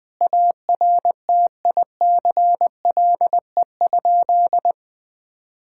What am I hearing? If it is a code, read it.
Morse code, heard as a keyed tone, about 20 wpm, 700 Hz. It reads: ARTICLE?
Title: Sentences from Top 1000 Words - Review Part 5 - 20wpm